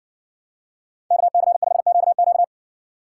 Morse code, heard as a keyed tone, about 65 wpm, 700 Hz. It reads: KC5QQ